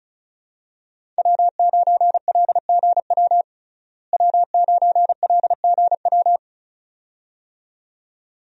Morse code, heard as a keyed tone, about 35 wpm, 700 Hz. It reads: W9LGW W9LGW
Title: USA Call Signs - 5 Characters - Repeated 2x - 35wpm